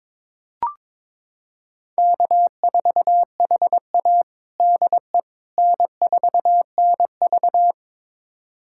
Morse code, heard as a keyed tone, about 22 wpm, 700 Hz. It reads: K4HA DE N4NV
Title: USA Callsign DE Callsign - 4 Characters - 22wpm